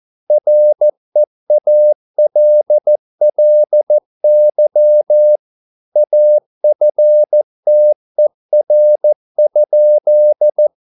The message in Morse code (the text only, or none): REALLY AFTER?